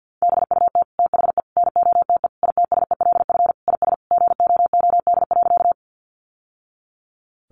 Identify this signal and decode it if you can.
Morse code, heard as a keyed tone, about 50 wpm, 700 Hz. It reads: BUT THE NOTE ITSELF IS GOOD.
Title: Sentences from Top 800 Words 50wpm